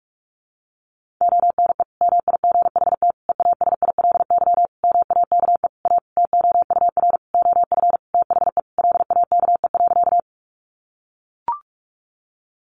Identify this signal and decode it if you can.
Morse code, heard as a keyed tone, about 44 wpm, 700 Hz. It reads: ONE MIGHT EASILY MAKE A TOUR OF THE LAKE.
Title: Sentences from Top 1400 Words 45wpm